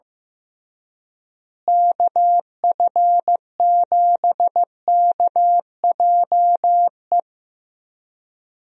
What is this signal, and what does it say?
Morse code, heard as a keyed tone, about 15 wpm, 700 Hz. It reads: KF7KJE
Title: USA Call Signs - 6 Characters - Repeated 2x - 15wpm